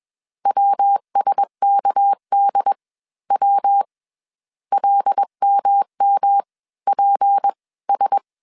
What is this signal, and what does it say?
Morse code, heard as a keyed tone, about 21 wpm, 800 Hz. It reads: WSKD W LMM PS
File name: CW / DET / 1 6 1500C S C 20wpm 2116